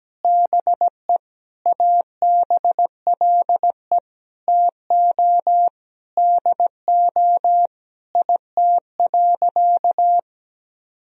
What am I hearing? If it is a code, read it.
Morse code, heard as a keyed tone, about 17 wpm, 700 Hz. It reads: BE ABLE TO DO IT.
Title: Sentences from Top 1000 Words - Review Part 2 - 17wpm